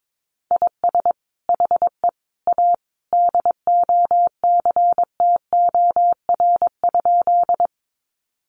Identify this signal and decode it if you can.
Morse code, heard as a keyed tone, about 22 wpm, 700 Hz. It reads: IS HE A DOCTOR?
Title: Sentences from Top 1500 Words 22wpm